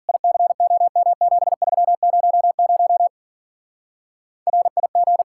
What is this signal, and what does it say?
Morse code, heard as a keyed tone, about 50 wpm, 700 Hz. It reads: ICOM7300 RIG